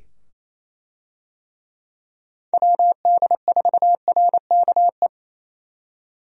Morse code, heard as a keyed tone, about 27 wpm, 700 Hz. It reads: WD4RKE
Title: USA Call Signs - 6 Characters - 28wpm